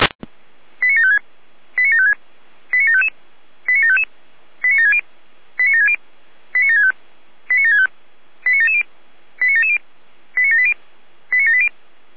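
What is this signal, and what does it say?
Five-tone selective call (ZVEI), heard as quick runs of five tones, tones of about 70 ms. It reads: 89756 89756 89755 89755 89760 89760 89765 89765 89770 89770 89780 89780
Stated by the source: ZVEI five-tone selective call received on SDR.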